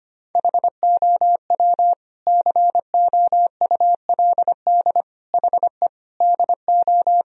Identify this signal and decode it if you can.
Morse code, heard as a keyed tone, about 25 wpm, 700 Hz. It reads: HOW COULD HE DO
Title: Sentences from Top 1000 Words - Review Part 1 - 25wpm